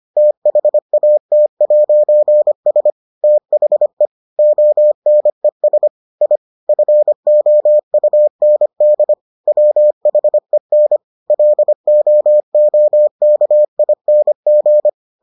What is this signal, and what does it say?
Morse code, heard as a keyed tone, about 25 wpm, 600 Hz. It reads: THAT'S THE ONES I FOUND WHEN LOOKING